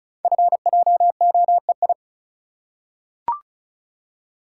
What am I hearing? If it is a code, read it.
Morse code, heard as a keyed tone, about 36 wpm, 690 Hz. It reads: FJOEI
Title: Code Groups - 5 Letters - 35wpm